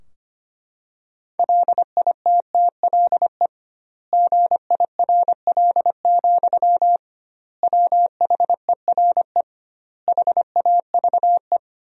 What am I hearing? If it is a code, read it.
Morse code, heard as a keyed tone, about 25 wpm, 700 Hz. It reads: LITTLE GIRL, WHERE HAVE
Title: 2x Word-Spacing - Sentences from Top 400 Words - 25wpm